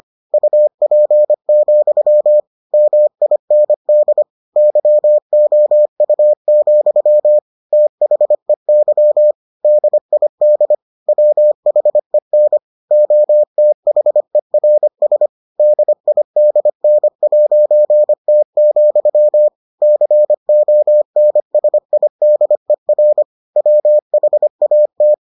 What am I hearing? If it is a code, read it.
Morse code, heard as a keyed tone, about 25 wpm, 600 Hz. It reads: UP, MIND YOU, THEY DID WHEN OTHERS DIDN'T, CONSIDER WHAT